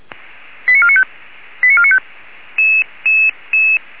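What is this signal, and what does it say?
Five-tone selective call (ZVEI), heard as quick runs of five tones, tones of about 70 ms. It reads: 89495 89495
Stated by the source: ZVEI five-tone selective call received by SDR.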